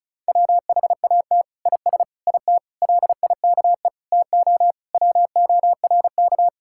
Morse code, heard as a keyed tone, about 36 wpm, 700 Hz. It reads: WHAT IS IT LIKE TO WORK
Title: Sentences from Top 1000 Words - Review Part 6 - 35wpm